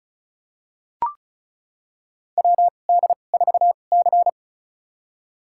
Morse code, heard as a keyed tone, about 34 wpm, spaced as for 25 wpm, 700 Hz. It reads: WD4C
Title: USA Call Signs - 2x Character Spacing - 35wpm